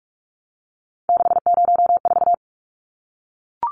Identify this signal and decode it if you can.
Morse code, heard as a keyed tone, about 44 wpm, 700 Hz. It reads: 604
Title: Numbers - 3-Digit 45wpm